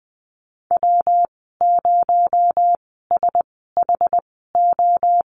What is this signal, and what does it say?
Morse code, heard as a keyed tone, about 20 wpm, spaced as for 14 wpm, 700 Hz. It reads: W0SHO
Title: 219 Lesson 35g - Number 0 - Call Signs 20wpm